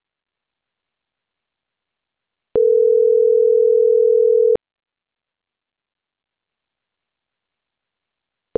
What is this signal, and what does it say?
Ringback tone (North America), 440 + 480 Hz.